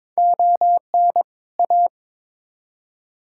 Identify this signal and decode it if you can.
Morse code, heard as a keyed tone, about 22 wpm, 700 Hz. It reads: ON A
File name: Build A Phrase - 2-Word Phrases - 22wpm